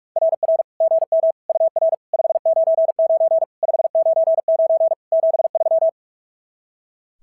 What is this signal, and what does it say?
Morse code, heard as a keyed tone, about 44 wpm, 690 Hz. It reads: RR GM UR 599 599 73
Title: SOTA QSO Practice 45wpm